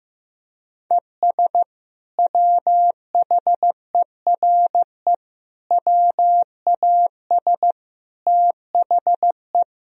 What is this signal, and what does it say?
Morse code, heard as a keyed tone, about 16 wpm, 700 Hz. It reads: ES WHERE WAS THE
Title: Sentences from Top 200 Words and Abbreviations 15wpm